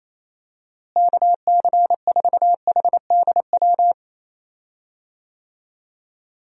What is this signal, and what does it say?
Morse code, heard as a keyed tone, about 28 wpm, 700 Hz. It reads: KC4HDW
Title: USA Call Signs - 6 Characters - Repeated 2x - 28wpm